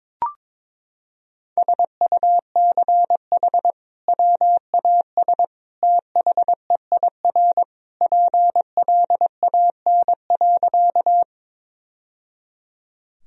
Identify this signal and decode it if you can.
Morse code, heard as a keyed tone, about 22 wpm, 700 Hz. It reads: SUCH WAS THEIR PLAN.
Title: Sentences from Top 1000 Words - Review Part 2 - 22wpm